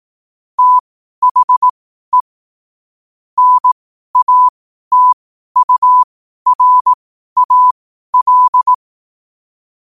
Morse code, heard as a keyed tone, about 18 wpm, spaced as for 12 wpm, 1000 Hz. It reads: THE NATURAL
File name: qotdfiles / 12wpm